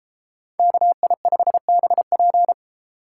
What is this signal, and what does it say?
Morse code, heard as a keyed tone, about 32 wpm, 700 Hz. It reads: KI5BP